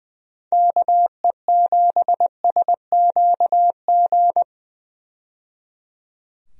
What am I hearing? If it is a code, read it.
Morse code, heard as a keyed tone, about 20 wpm, 700 Hz. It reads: KE7SQG
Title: USA Call Signs - 6 Characters - 20wpm